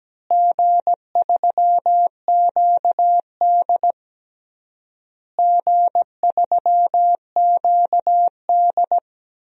Morse code, heard as a keyed tone, about 17 wpm, 700 Hz. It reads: G3QD G3QD
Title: DXCC-100 Call Signs - 4 Characters - Repeated 3x - 17wpm